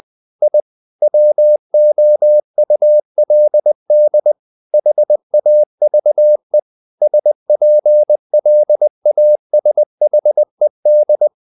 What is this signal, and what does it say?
Morse code, heard as a keyed tone, about 20 wpm, 600 Hz. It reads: I WOULD HAVE SPLASHED
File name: the old man and the sea part 2 20wpm